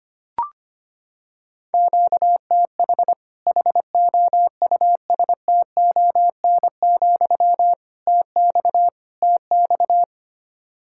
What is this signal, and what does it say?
Morse code, heard as a keyed tone, about 25 wpm, 700 Hz. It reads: QTH HOUSTON, TX TX
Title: Practice QSO - Exchange 1 - Beginning - 25wpm